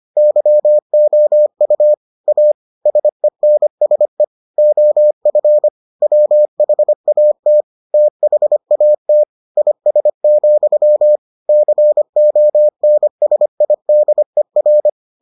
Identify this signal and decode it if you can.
Morse code, heard as a keyed tone, about 25 wpm, 600 Hz. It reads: YOU A SENSE OF WHAT THAT IS, CONSIDER